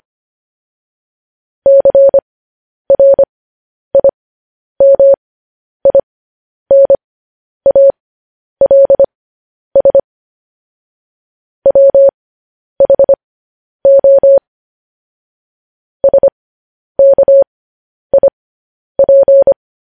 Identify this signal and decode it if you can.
Morse code, heard as a keyed tone, about 25 wpm, spaced as for 10 wpm, 550 Hz. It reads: CRIMINALS WHO SKIP
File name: podcast / News headlines-1x-25x10